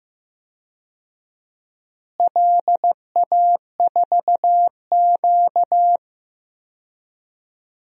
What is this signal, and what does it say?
Morse code, heard as a keyed tone, about 15 wpm, 700 Hz. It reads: LA4Q
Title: DXCC-100 Call Signs - 4 Characters - Repeated 2x - 15wpm